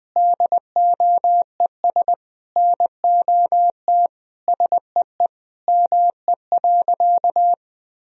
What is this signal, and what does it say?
Morse code, heard as a keyed tone, about 20 wpm, 700 Hz. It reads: DOES NOT SEE ME.
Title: Sentences from Top 1000 Words - Review Part 3 - 20wpm